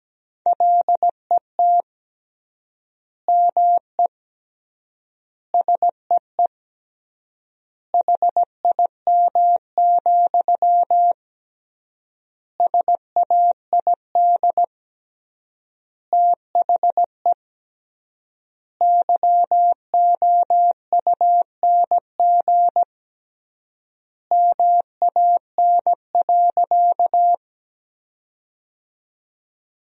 Morse code, heard as a keyed tone, about 17 wpm, 700 Hz. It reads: LET ME SEE HIM, SAID THE YOUNG MAN.
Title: 3x Word-Spacing - Sentences from Top 400 Words - 17wpm